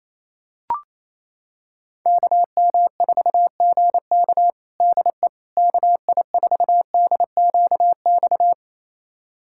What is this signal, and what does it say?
Morse code, heard as a keyed tone, about 27 wpm, 700 Hz. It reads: KM4GK DE KI4DQX
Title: USA Callsign DE Callsign 28wpm